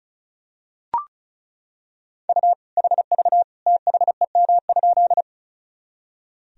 Morse code, heard as a keyed tone, about 34 wpm, 700 Hz. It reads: U HV THEM?
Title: Sentences from Top 100 Words and Abbreviations 35wpm